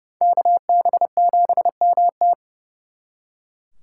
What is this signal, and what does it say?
Morse code, heard as a keyed tone, about 36 wpm, 700 Hz. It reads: KB7MT
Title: Speed Racing - USA Call Signs 30wpm